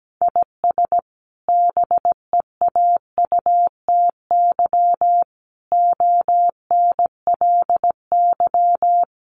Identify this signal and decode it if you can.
Morse code, heard as a keyed tone, about 17 wpm, 700 Hz. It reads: IS BEAUTY ONLY
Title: Sentences from Top 1300 Words 17wpm